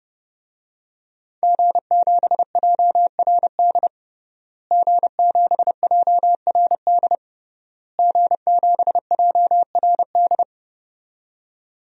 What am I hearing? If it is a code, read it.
Morse code, heard as a keyed tone, about 30 wpm, 700 Hz. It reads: G7JRD G7JRD G7JRD
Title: DXCC-100 Call Signs - 5 Characters - Repeated 3x - 30wpm